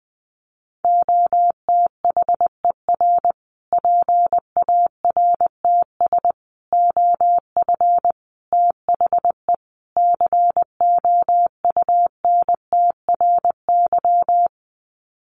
Morse code, heard as a keyed tone, about 20 wpm, 700 Hz. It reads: OTHER PARTS OF THE COUNTRY
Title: Sets of 5 Words Encore 20wpm